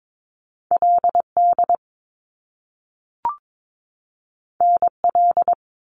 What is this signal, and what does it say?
Morse code, heard as a keyed tone, about 22 wpm, 700 Hz. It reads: LD NL
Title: Top Two Letter Combinations in Words 22wpm